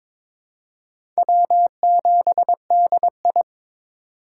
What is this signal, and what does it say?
Morse code, heard as a keyed tone, about 22 wpm, 700 Hz. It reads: W7DI
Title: USA Call Signs - 4 Characters - 2x Character Spacing - 22wpm